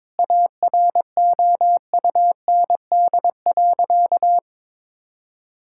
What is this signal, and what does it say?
Morse code, heard as a keyed tone, about 22 wpm, 700 Hz. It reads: AROUND.
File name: practice / 22wpm / Sentences with 2-Word Phrases - 22wpm